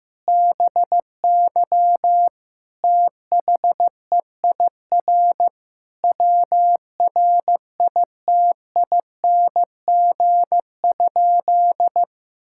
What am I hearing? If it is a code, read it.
Morse code, heard as a keyed tone, about 16 wpm, 700 Hz. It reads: BY THEIR WRITING?
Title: Sentences from Top 1000 Words - Review Part 6 - 15wpm